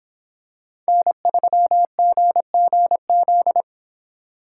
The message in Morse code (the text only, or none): N3GGZ